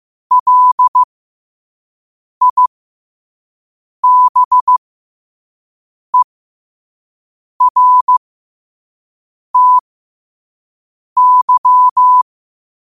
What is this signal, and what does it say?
Morse code, heard as a keyed tone, about 15 wpm, spaced as for 5 wpm, 1000 Hz. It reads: LIBERTY
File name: qotdfiles / 5wpm